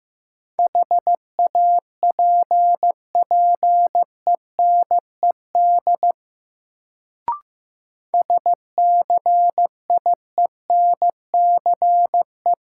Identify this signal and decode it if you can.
Morse code, heard as a keyed tone, about 15 wpm, 700 Hz. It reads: HAPPENED SCIENCE
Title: Sentences from Top 1000 Words - Review Part 2 - New Word Drill - 15wpm